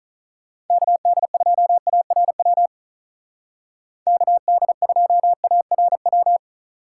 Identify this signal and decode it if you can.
Morse code, heard as a keyed tone, about 45 wpm, 700 Hz. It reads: KD2ARW KD2ARW